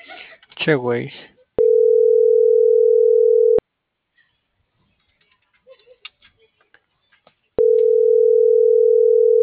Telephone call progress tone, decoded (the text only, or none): ringback tone